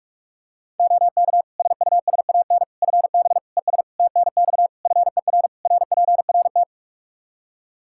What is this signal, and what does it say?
Morse code, heard as a keyed tone, about 42 wpm, 700 Hz. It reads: OK SUSAN FB ES TNX FER RPRT
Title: Practice QSO - Exchange 2 - Info - 45wpm